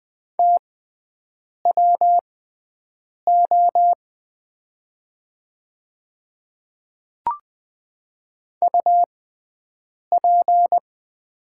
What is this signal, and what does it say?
Morse code, heard as a keyed tone, about 20 wpm, spaced as for 7 wpm, 700 Hz. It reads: TWO UP
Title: 6x Character Spacing - Top 200 Words - 20wpm